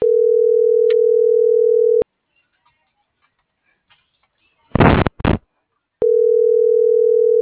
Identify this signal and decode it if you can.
Ringback tone (North America), 440 + 480 Hz.